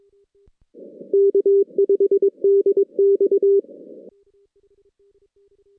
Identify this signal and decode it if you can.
Morse code, heard as a keyed tone, about 22 wpm, 400 Hz. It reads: K5DX
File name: 2016 JAN CW NAQP / 2016 Jan naqp k5ot - 0-10